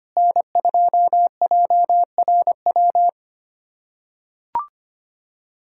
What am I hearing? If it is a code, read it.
Morse code, heard as a keyed tone, about 25 wpm, 700 Hz. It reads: N2JRW
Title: USA Callsign DE Callsign - 5 Characters - 25wpm